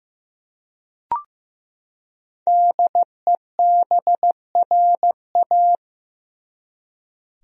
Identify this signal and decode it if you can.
Morse code, heard as a keyed tone, about 15 wpm, 700 Hz. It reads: DEBRA
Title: USA Common Names - 15wpm